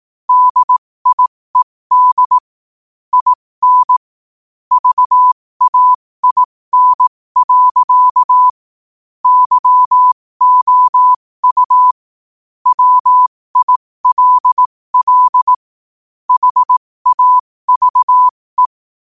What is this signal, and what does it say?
Morse code, heard as a keyed tone, about 18 wpm, spaced as for 15 wpm, 1000 Hz. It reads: DIED IN VAIN. YOU WILL HAVE